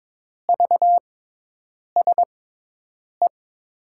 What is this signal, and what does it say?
Morse code, heard as a keyed tone, about 22 wpm, spaced as for 8 wpm, 700 Hz. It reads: VSE
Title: USA Call Signs - 5 Characters - 6x Character Spacing - 22wpm